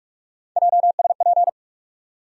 Morse code, heard as a keyed tone, about 46 wpm, 700 Hz. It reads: JSP
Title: USA Call Sign Postfixes - 3 Letters 45wpm